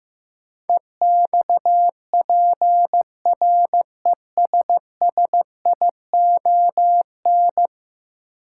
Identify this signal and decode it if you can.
Morse code, heard as a keyed tone, about 15 wpm, 700 Hz. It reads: EXPRESSION
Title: Sentences from Top 1200 Words - New Word Drill - 15wpm